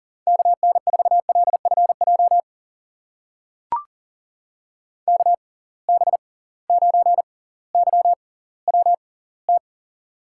Morse code, heard as a keyed tone, about 39 wpm, spaced as for 20 wpm, 700 Hz. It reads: KN4LFJ KB9YWT
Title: USA Call Signs - 6 Characters - 6x Character Spacing - 40wpm